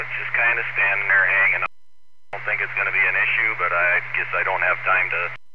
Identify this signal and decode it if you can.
Emergency Alert System (SAME) tones, heard as harsh buzzing bursts, with speech. It is heard from 0.9 to 4.8 s.